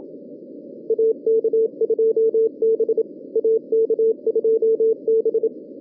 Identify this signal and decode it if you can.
Morse code, heard as a keyed tone, about 27 wpm, 450 Hz, with radio noise. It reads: AK2B AK2B